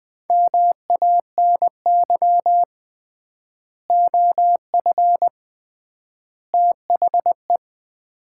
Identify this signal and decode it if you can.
Morse code, heard as a keyed tone, about 20 wpm, 700 Hz. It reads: MANY OF THE
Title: 3x Word-Spacing - Sets of 3 Words - 20wpm